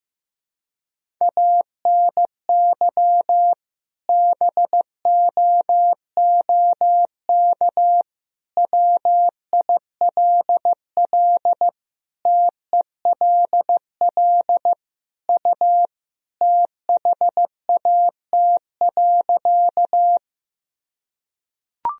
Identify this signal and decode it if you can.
Morse code, heard as a keyed tone, about 15 wpm, 700 Hz. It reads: ANY BOOK WILL TELL U THAT.
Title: Sentences from Top 300 Words and Abbreviations 15wpm